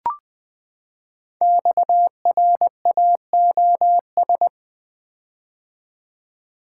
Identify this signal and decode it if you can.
Morse code, heard as a keyed tone, about 20 wpm, 700 Hz. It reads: XRAOS
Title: Code Groups - 5 Letters - 20wpm